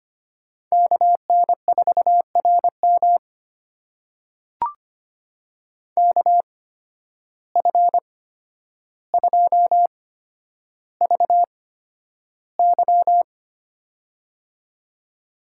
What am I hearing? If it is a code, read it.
Morse code, heard as a keyed tone, about 25 wpm, spaced as for 11 wpm, 700 Hz. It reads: KN4RM KF2VY
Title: USA Call Signs - 8x Character Spacing - 25wpm